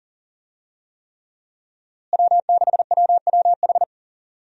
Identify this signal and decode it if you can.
Morse code, heard as a keyed tone, about 40 wpm, 700 Hz. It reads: W6WWH
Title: USA Call Signs - 5 Characters - Repeated 3x - 40wpm